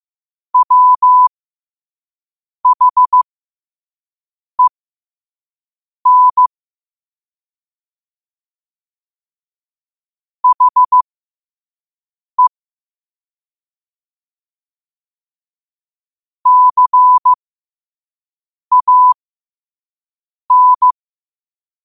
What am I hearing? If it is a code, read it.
Morse code, heard as a keyed tone, about 15 wpm, spaced as for 5 wpm, 1000 Hz. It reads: WHEN HE CAN
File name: qotdfiles / 5wpm